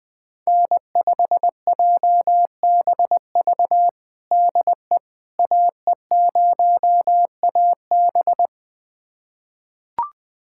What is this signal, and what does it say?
Morse code, heard as a keyed tone, about 20 wpm, 700 Hz. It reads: N5JBV DE AE0AB
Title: USA Callsign DE Callsign - 5 Characters - 20wpm